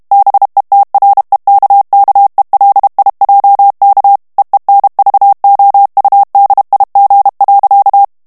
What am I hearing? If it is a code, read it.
Morse code, heard as a keyed tone, about 32 wpm, 800 Hz. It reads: BETREKKELIJK EENVOUDIG.
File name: HOLLANDAIS texte clair 2 vitesse 30 mots minute